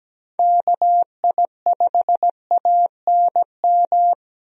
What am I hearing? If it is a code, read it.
Morse code, heard as a keyed tone, about 17 wpm, 700 Hz. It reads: KI5ANM
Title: USA Call Signs - 6 Characters - Repeated 3x - 17wpm